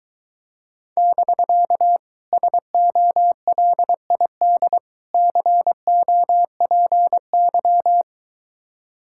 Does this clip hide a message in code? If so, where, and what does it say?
Yes: Morse code from 1.0 to 8.0 s, heard as a keyed tone, about 22 wpm, 700 Hz. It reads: <BK> SOLID COPY